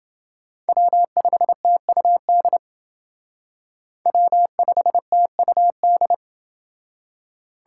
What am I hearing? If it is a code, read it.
Morse code, heard as a keyed tone, about 28 wpm, 700 Hz. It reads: W5TUD W5TUD